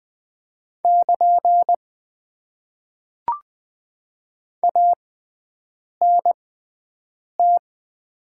Morse code, heard as a keyed tone, about 20 wpm, spaced as for 7 wpm, 700 Hz. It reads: <KN> ANT
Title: Top 90 QSO Elements - 6x Character Spacing - 20wpm